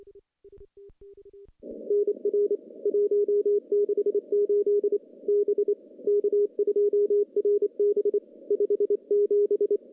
Morse code, heard as a keyed tone, about 27 wpm, 410 Hz. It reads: NR 168 B K2RB 57
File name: contests / CW-SS-k5tr- -0400